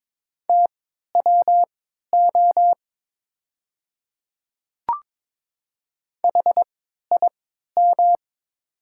Morse code, heard as a keyed tone, about 23 wpm, spaced as for 13 wpm, 700 Hz. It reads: TWO HIM
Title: 3x Character Spacing - Top 100 Words - 22wpm